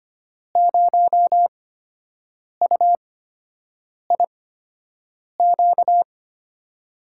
Morse code, heard as a keyed tone, about 25 wpm, spaced as for 7 wpm, 700 Hz. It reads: 0UIQ
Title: USA Call Signs - 5 Characters - 8x Character Spacing - 25wpm